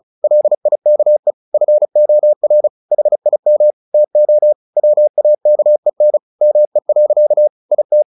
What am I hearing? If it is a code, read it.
Morse code, heard as a keyed tone, about 34 wpm, 600 Hz. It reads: LIKE FOR HIM TO WAKEN ME. IT
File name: the old man and the sea / the old man and the sea 1 35wpm